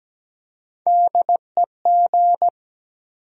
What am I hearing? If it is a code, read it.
Morse code, heard as a keyed tone, about 17 wpm, 700 Hz. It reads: DEG
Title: USA Call Sign Postfixes - 3 Letters 17wpm